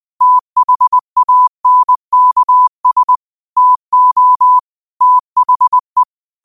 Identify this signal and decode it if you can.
Morse code, heard as a keyed tone, about 20 wpm, 1000 Hz. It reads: THANKS TO THE
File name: qotdfiles / 20wpm